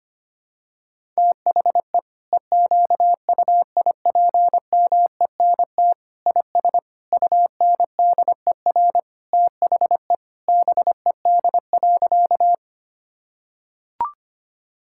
Morse code, heard as a keyed tone, about 25 wpm, 700 Hz. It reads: THE EQUIPMENT IS UNDER THE BED.
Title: Sentences from Top 1000 Words - Review Part 5 - 25wpm